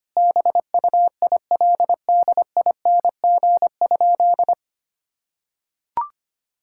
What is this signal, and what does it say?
Morse code, heard as a keyed tone, about 25 wpm, 700 Hz. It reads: BUILDING?
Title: Sentences from Top 1000 Words - Review Part 5 - 25wpm